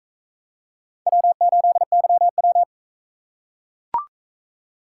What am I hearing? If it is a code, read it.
Morse code, heard as a keyed tone, about 39 wpm, 700 Hz. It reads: W8YW